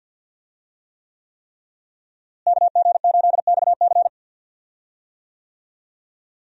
Morse code, heard as a keyed tone, about 46 wpm, 700 Hz. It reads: KG8XC
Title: USA Call Signs - 5 Characters - Repeated 3x - 50wpm